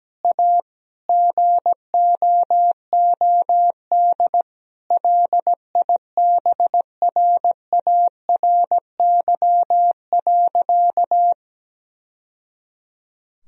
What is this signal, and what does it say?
Morse code, heard as a keyed tone, about 17 wpm, 700 Hz. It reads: A GOOD LIBRARY.
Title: Sentences from Top 1000 Words - Review Part 4 - 17wpm